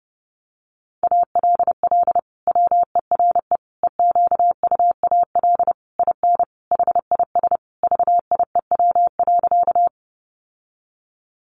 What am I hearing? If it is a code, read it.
Morse code, heard as a keyed tone, about 30 wpm, 700 Hz. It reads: ALL WERE EQUAL IN HIS VIEW.
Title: Sentences from Top 1400 Words 30wpm